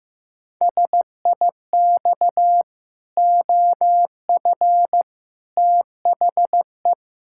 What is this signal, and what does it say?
Morse code, heard as a keyed tone, about 15 wpm, 700 Hz. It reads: SIX OF THE
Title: Pangram Challenge v2 - 15wpm